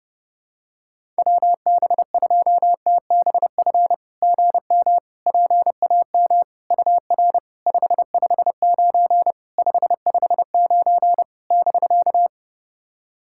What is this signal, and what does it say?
Morse code, heard as a keyed tone, about 30 wpm, 690 Hz. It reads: WB2TBF GM PAM UR 559 559 <BK>
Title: SOTA QSO Practice 30wpm